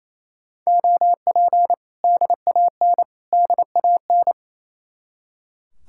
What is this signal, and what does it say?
Morse code, heard as a keyed tone, about 34 wpm, 700 Hz. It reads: OP DAN DAN
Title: Speed-Racing Common QSO Phrases - 28wpm